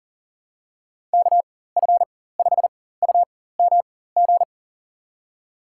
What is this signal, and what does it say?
Morse code, heard as a keyed tone, about 40 wpm, spaced as for 19 wpm, 700 Hz. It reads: KF5UMG
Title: USA Call Signs - 6 Characters - 4x Character Spacing - 40wpm